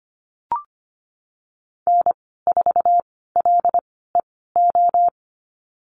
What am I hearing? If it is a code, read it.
Morse code, heard as a keyed tone, about 25 wpm, spaced as for 16 wpm, 700 Hz. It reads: N4LEO
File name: Lesson 15g - Number 5 - Call Signs 25wpm